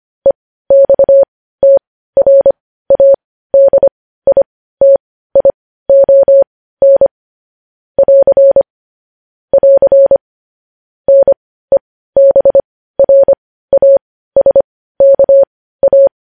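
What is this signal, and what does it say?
Morse code, heard as a keyed tone, about 25 wpm, spaced as for 15 wpm, 550 Hz. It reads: EXTRADITION <AR> <AR> NEBRASKA
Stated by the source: Morse code news headlines broadcast.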